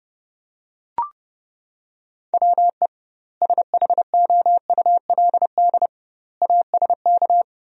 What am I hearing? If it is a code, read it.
Morse code, heard as a keyed tone, about 30 wpm, 700 Hz. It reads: WE SHOULD ASK
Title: 2x Word-Spacing - Sentences from Top 300 Words - 30wpm